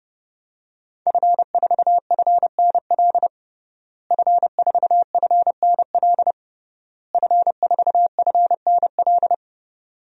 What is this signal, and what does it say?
Morse code, heard as a keyed tone, about 32 wpm, 700 Hz. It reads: F4FNL F4FNL F4FNL
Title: DXCC-100 Call Signs - 5 Characters - Repeated 3x - 30wpm